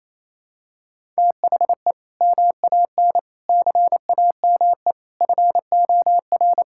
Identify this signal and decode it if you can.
Morse code, heard as a keyed tone, about 28 wpm, 700 Hz. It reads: THE MAN CAME FOR
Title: Sentences from Top 1000 Words - Review Part 5 - 28wpm